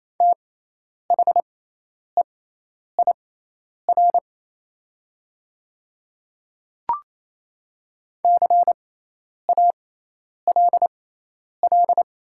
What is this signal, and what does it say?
Morse code, heard as a keyed tone, about 27 wpm, spaced as for 10 wpm, 700 Hz. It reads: THEIR CALL
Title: 6x Character Spacing - Top 100 Words - 28wpm